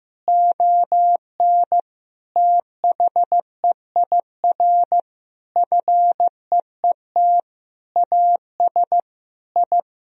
Morse code, heard as a keyed tone, about 15 wpm, 700 Hz. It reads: ON THEIR FEET AS I
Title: Sentences from Top 1000 Words - Review Part 1 - 15wpm